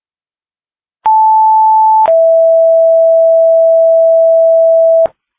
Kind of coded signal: two-tone page